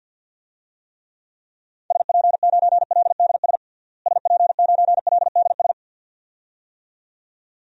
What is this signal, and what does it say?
Morse code, heard as a keyed tone, about 51 wpm, 700 Hz. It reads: SP9LDS SP9LDS
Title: DXCC-100 Call Signs - 6 Characters - Repeated 2x - 50wpm